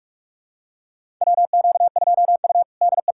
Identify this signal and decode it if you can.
Morse code, heard as a keyed tone, about 46 wpm, 700 Hz. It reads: WQ2U DE
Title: USA Callsign DE Callsign - 4 Characters - 45wpm